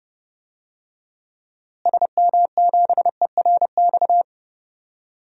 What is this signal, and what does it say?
Morse code, heard as a keyed tone, about 30 wpm, 700 Hz. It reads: SM7ERX
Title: DXCC-100 Call Signs - 6 Characters - Repeated 3x - 30wpm